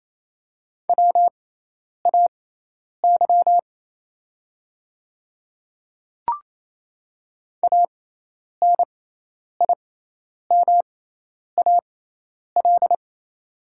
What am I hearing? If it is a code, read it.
Morse code, heard as a keyed tone, about 28 wpm, spaced as for 10 wpm, 700 Hz. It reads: WAY ANIMAL
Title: 6x Character Spacing - Top 200 Words - 28wpm